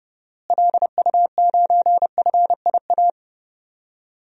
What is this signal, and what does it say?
Morse code, heard as a keyed tone, about 32 wpm, 700 Hz. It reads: LU9FIA
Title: DXCC-100 Call Signs - 6 Characters - Repeated 3x - 30wpm